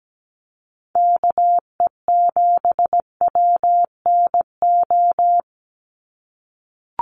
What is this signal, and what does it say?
Morse code, heard as a keyed tone, about 17 wpm, 700 Hz. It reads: KE7WNO
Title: USA Call Signs 17wpm